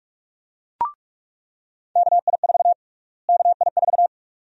Morse code, heard as a keyed tone, about 46 wpm, 700 Hz. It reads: KI4 KI4
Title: USA Call Sign Prefixes - Repeated 2x - 45wpm